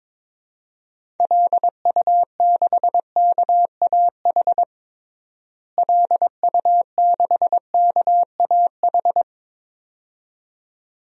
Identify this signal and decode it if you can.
Morse code, heard as a keyed tone, about 22 wpm, 700 Hz. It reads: LU6KAH LU6KAH
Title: DXCC-100 Call Signs - 6 Characters - Repeated 2x - 22wpm